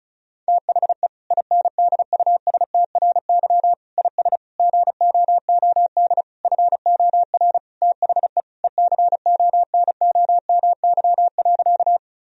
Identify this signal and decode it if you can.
Morse code, heard as a keyed tone, about 36 wpm, 700 Hz. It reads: THE INDUSTRY IS GOOD FOR THE ECONOMY.
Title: Sentences from Top 1000 Words - Review Part 4 - 35wpm